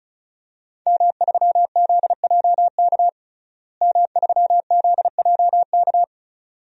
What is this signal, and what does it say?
Morse code, heard as a keyed tone, about 34 wpm, 700 Hz. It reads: M3ZJK M3ZJK
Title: DXCC-100 Call Signs - 5 Characters - Repeated 2x - 35wpm